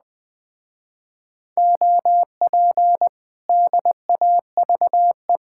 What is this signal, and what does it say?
Morse code, heard as a keyed tone, about 20 wpm, 700 Hz. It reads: OP DAVE
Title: Speed-Racing Common QSO Phrases - 17wpm